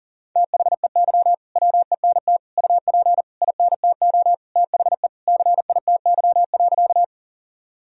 Morse code, heard as a keyed tone, about 40 wpm, 700 Hz. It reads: THEY WENT UP INTO THE CITY.
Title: Sentences from Top 1000 Words - Review Part 1 - 40wpm